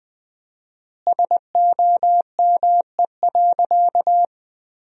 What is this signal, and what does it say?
Morse code, heard as a keyed tone, about 21 wpm, 700 Hz. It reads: SOME.
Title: 4x Word-Spacing - Sentences from Top 100 Words - 20wpm